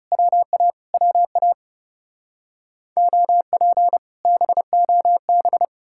Morse code, heard as a keyed tone, about 40 wpm, 700 Hz. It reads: WA WA OP BOB